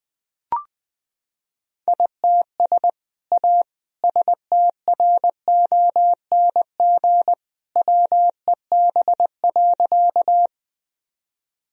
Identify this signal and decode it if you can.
Morse code, heard as a keyed tone, about 20 wpm, 700 Hz. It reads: ITS A STRONG WEB.
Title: Sentences from Top 1000 Words - Review Part 4 - 20wpm